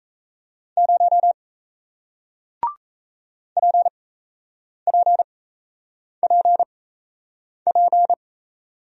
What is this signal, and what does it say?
Morse code, heard as a keyed tone, about 38 wpm, spaced as for 8 wpm, 700 Hz. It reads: 0 PPPP